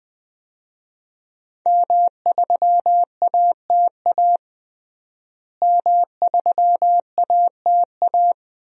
Morse code, heard as a keyed tone, about 20 wpm, 700 Hz. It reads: M3ATA M3ATA
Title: DXCC-100 Call Signs - 5 Characters - Repeated 3x - 20wpm